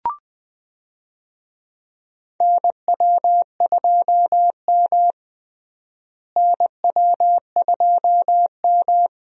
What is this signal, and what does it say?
Morse code, heard as a keyed tone, about 20 wpm, 700 Hz. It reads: NW2M NW2M
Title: USA Call Signs - 4 Characters - Repeated 2x - 20wpm